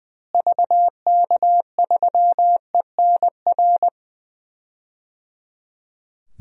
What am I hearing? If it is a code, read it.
Morse code, heard as a keyed tone, about 20 wpm, 700 Hz. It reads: VK3ENR
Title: DXCC-100 Call Signs - 6 Characters - Repeated 3x - 20wpm